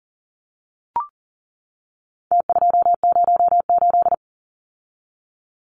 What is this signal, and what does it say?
Morse code, heard as a keyed tone, about 40 wpm, 700 Hz. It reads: T208
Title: Numbers - Contest Style - 4-Digit 40wpm